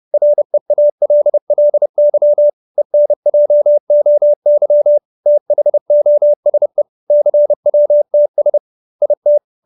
Morse code, heard as a keyed tone, about 30 wpm, 590 Hz. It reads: REALLY ENJOY THOSE CWTS IT
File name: QSO316 30